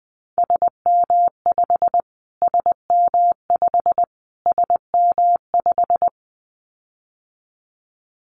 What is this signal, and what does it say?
Morse code, heard as a keyed tone, about 20 wpm, 700 Hz. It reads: SM5 SM5 SM5
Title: Callsign Prefixes for Top 25 DXCC - Repeated 3x - 20wpm